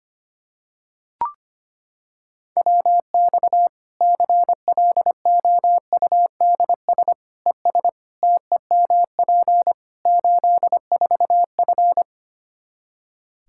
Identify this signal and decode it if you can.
Morse code, heard as a keyed tone, about 25 wpm, 700 Hz. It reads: WX CLOUDS ES TEMP 84F
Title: Practice QSO - Exchange 2 - Info - 25wpm